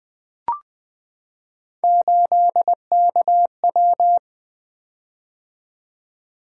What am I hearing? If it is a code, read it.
Morse code, heard as a keyed tone, about 20 wpm, 700 Hz. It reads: 8KW
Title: Three Letter-Number Combinations - 20wpm